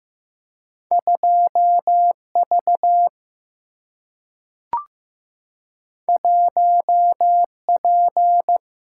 Morse code, heard as a keyed tone, about 15 wpm, 700 Hz. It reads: 2 V 1 P
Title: Single-Letter Single-Number 15wpm